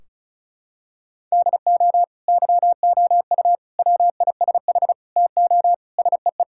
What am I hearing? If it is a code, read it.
Morse code, heard as a keyed tone, about 36 wpm, 700 Hz. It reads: DO YOU WISH TO SEE
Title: Sentences from Top 1600 Words 35wpm